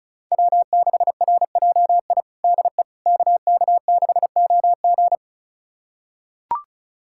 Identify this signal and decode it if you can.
Morse code, heard as a keyed tone, about 36 wpm, 700 Hz. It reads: W6RJI DE KK6OG
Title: USA Callsign DE Callsign - 5 Characters - 35wpm